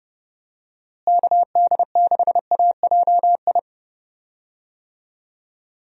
Morse code, heard as a keyed tone, about 31 wpm, 700 Hz. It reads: KD6AJI
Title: USA Call Signs - 6 Characters - Repeated 2x - 30wpm